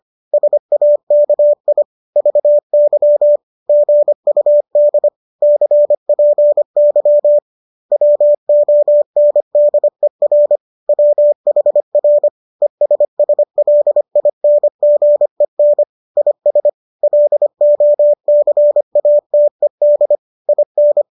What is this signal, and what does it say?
Morse code, heard as a keyed tone, about 25 wpm, 590 Hz. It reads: SAKI VY GUD CPY WONDER WHR ESSLINGEN IS LOCATED IN
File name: qso218 25